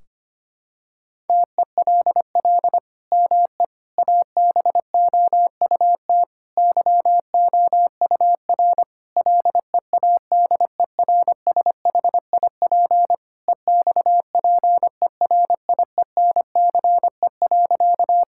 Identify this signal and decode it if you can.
Morse code, heard as a keyed tone, about 25 wpm, 700 Hz. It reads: TELL ME ABOUT YOUR LEADERSHIP EXPERIENCE.
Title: Sentences from Top 1600 Words 25wpm